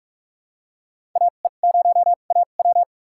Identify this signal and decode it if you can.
Morse code, heard as a keyed tone, about 46 wpm, spaced as for 33 wpm, 700 Hz. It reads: AE0AW
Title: USA Call Signs - 5 Characters - 2x Character Spacing - 45wpm